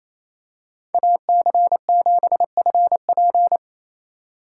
Morse code, heard as a keyed tone, about 28 wpm, 700 Hz. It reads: AC7FP
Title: USA Call Signs - 5 Characters - 4x Character Spacing - 28wpm